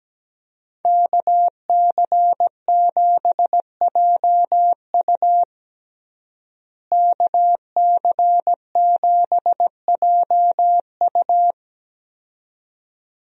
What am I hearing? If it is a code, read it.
Morse code, heard as a keyed tone, about 17 wpm, 700 Hz. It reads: KC7JU KC7JU
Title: USA Call Signs - 5 Characters - Repeated 2x - 17wpm